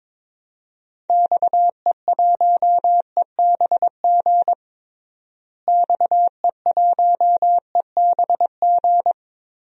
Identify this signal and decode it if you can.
Morse code, heard as a keyed tone, about 22 wpm, 700 Hz. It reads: XE1EBG XE1EBG
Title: DXCC-100 Call Signs - 6 Characters - Repeated 2x - 22wpm